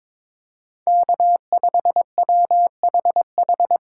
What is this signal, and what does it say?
Morse code, heard as a keyed tone, about 25 wpm, 700 Hz. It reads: K5WHH
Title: Speed Racing - USA 1x3 Call Signs 22wpm